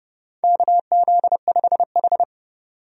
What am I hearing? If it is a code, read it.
Morse code, heard as a keyed tone, about 30 wpm, 700 Hz. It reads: KZ5H
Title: USA Call Signs - 4 Characters - 30wpm